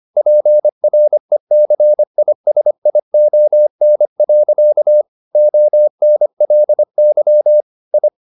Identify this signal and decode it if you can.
Morse code, heard as a keyed tone, about 25 wpm, 600 Hz. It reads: PRECISION. ONLY I
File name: the old man and the sea part 1 25wpm